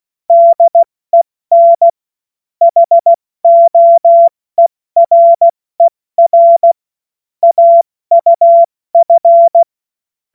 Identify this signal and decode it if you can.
Morse code, heard as a keyed tone, about 16 wpm, 680 Hz. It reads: DEN HOERER AUF
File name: Morsekurs / Witze / Witz 10. Mai 2022 70 F80